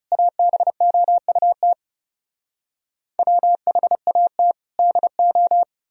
Morse code, heard as a keyed tone, about 40 wpm, 700 Hz. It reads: ABOUT WHAT DO